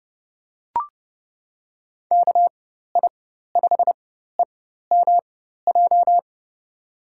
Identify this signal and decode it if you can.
Morse code, heard as a keyed tone, about 31 wpm, spaced as for 14 wpm, 700 Hz. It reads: KI5EMJ
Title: USA Call Signs - 6 Characters - 4x Character Spacing - 30wpm